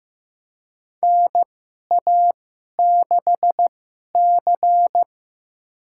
Morse code, heard as a keyed tone, about 15 wpm, spaced as for 11 wpm, 700 Hz. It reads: NA6C
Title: USA Call Signs - 4 Characters - 2x Character Spacing - 15wpm